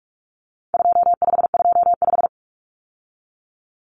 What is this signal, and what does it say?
Morse code, heard as a keyed tone, about 44 wpm, 700 Hz. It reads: 2525
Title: Common Cut Numbers - 4-Digit 45wpm